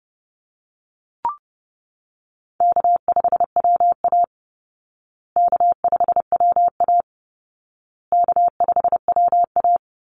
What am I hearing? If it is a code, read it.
Morse code, heard as a keyed tone, about 30 wpm, 700 Hz. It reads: K5WA K5WA K5WA
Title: Easy Call Signs Repeated 3x - 30wpm